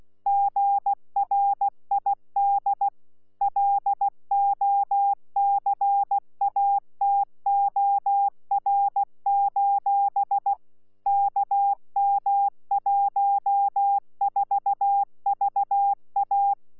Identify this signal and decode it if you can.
Morse code, heard as a keyed tone, about 16 wpm, 800 Hz. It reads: GRID LOCATOR: KM14VA
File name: Z PI4VRZ / PI4VRZ 14-jun-2025